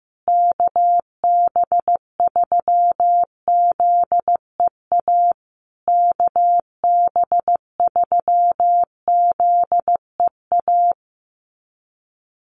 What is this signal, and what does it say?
Morse code, heard as a keyed tone, about 15 wpm, 700 Hz. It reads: KB3ZEA KB3ZEA
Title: USA Call Signs Repeated 3x 15wpm